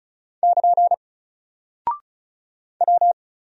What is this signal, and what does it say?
Morse code, heard as a keyed tone, about 36 wpm, 700 Hz. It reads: <KN> W
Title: Single Character (Letter:Number:Punctuation:Prosign) 35wpm